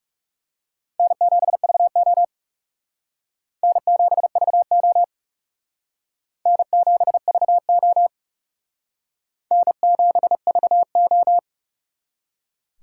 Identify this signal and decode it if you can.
Morse code, heard as a keyed tone, about 36 wpm, 700 Hz. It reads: N7VO N7VO N7VO N7VO
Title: Speed Racing - USA 1x2 Call Signs 30wpm